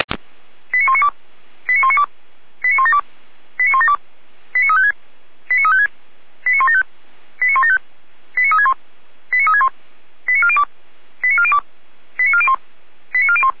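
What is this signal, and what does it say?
Five-tone selective call (ZVEI), heard as quick runs of five tones, tones of about 70 ms. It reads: 89192 89192 89172 89172 89356 89356 89175 89175 89361 89361 89402 89402 89401 89401
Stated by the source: ZVEI five-tone call received on SDR, code 89340.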